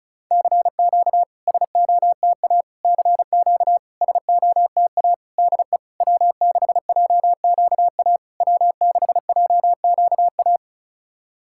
Morse code, heard as a keyed tone, about 36 wpm, 700 Hz. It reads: CQ SOTA CQ SOTA DE W6JQA W6JQA
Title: SOTA QSO Practice 35wpm